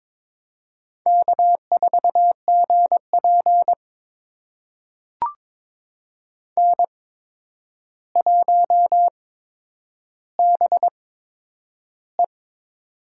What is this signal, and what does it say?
Morse code, heard as a keyed tone, about 22 wpm, spaced as for 9 wpm, 700 Hz. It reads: K4GP N1BE
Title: USA Call Signs - 4 Characters - 8x Character Spacing - 22wpm